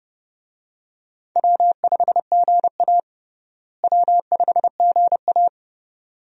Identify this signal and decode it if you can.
Morse code, heard as a keyed tone, about 29 wpm, 700 Hz. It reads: W5GA W5GA
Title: USA Call Signs - 4 Characters - Repeated 2x - 30wpm